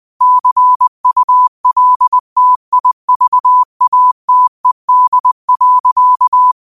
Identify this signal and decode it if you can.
Morse code, heard as a keyed tone, about 20 wpm, 1000 Hz. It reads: CULTIVATED.
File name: qotdfiles / 20wpm